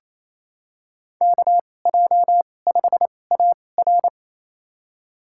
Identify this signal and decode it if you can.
Morse code, heard as a keyed tone, about 28 wpm, spaced as for 20 wpm, 700 Hz. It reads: KJ5AR
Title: USA Call Signs - 2x Character Spacing - 28wpm